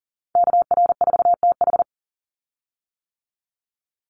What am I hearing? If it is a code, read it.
Morse code, heard as a keyed tone, about 40 wpm, 700 Hz. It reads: KR4TH
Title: Call Signs - 40wpm